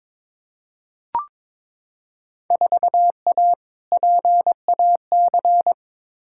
Morse code, heard as a keyed tone, about 22 wpm, 700 Hz. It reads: 4A PAC
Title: Field Day Exchanges Encore - 22wpm